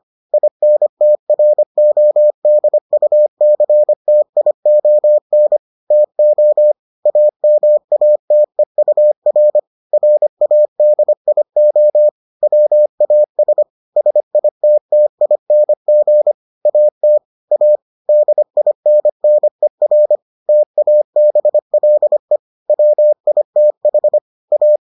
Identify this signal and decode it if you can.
Morse code, heard as a keyed tone, about 25 wpm, 600 Hz. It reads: INTRODUCTION TO AMATEUR RADIO WAS SITTING AT A DINNER TABLE WITH A